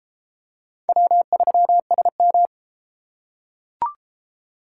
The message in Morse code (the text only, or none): W3SM